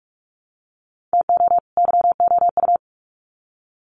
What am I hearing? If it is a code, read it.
Morse code, heard as a keyed tone, about 44 wpm, 700 Hz. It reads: TO YOU
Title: Sets of 2 Words 45wpm